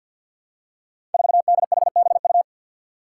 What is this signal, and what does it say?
Morse code, heard as a keyed tone, about 51 wpm, 700 Hz. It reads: 4DHBU
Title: Five Challenging Letter-Number Combinations 50wpm